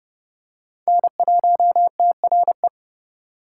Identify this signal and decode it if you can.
Morse code, heard as a keyed tone, about 29 wpm, 700 Hz. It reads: N1TRE